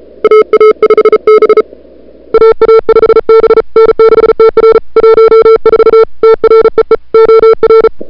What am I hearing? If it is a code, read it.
Morse code, heard as a keyed tone, about 34 wpm, 440 Hz. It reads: AA5B AA5B N6TR 14 TREE OR
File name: CWSprint / February 2007 CW Sprint / 01-2007 CW Sprint Hour1